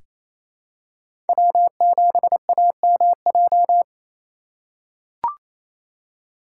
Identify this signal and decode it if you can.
Morse code, heard as a keyed tone, about 28 wpm, 700 Hz. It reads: W7AMJ
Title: USA Call Signs - 5 Characters - 2x Character Spacing - 28wpm